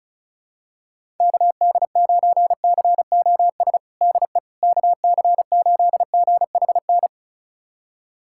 Morse code, heard as a keyed tone, about 36 wpm, 700 Hz. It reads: KD9COS DE KC8GHN
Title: USA Callsign DE Callsign - 6 Characters - 35wpm